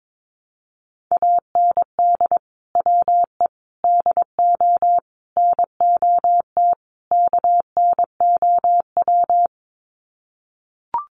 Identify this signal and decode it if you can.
Morse code, heard as a keyed tone, about 22 wpm, 700 Hz. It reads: AND WE DO NOT KNOW
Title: Sets of 5 Words Encore 22wpm